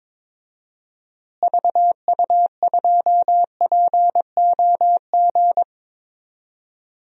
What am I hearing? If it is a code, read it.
Morse code, heard as a keyed tone, about 22 wpm, 700 Hz. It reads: VU2POG
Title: DXCC-100 Call Signs - 6 Characters - Repeated 3x - 22wpm